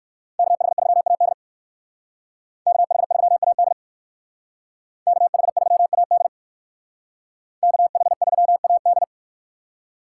Morse code, heard as a keyed tone, about 58 wpm, 700 Hz. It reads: KH3AD KH3AD KH3AD KH3AD